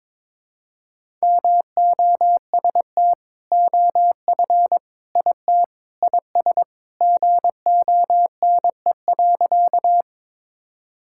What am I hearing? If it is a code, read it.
Morse code, heard as a keyed tone, about 22 wpm, 700 Hz. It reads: MOST OF IT IS GONE.
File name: practice / 22wpm / Sentences with 2-Word Phrases - 22wpm